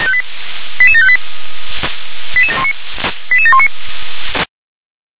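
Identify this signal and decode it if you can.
Five-tone selective call (ZVEI), heard as quick runs of five tones, tones of about 70 ms. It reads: 88758 88719
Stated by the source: ZVEI five-tone selective call received on an SDR.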